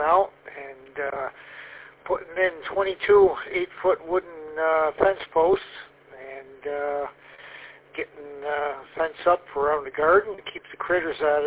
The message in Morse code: none

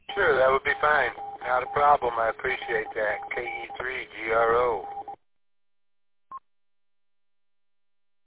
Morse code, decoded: none

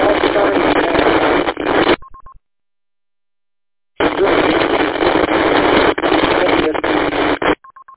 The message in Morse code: none

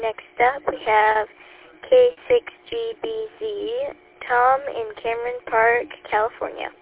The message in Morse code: none

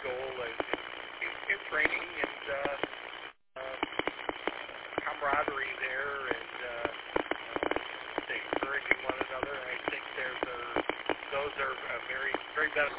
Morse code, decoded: none

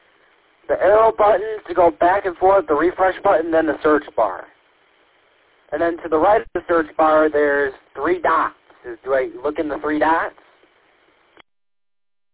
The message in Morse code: none

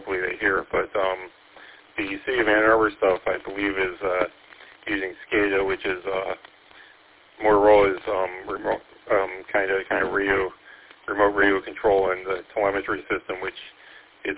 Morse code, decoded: none

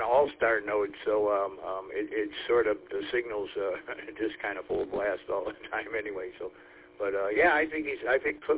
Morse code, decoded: none

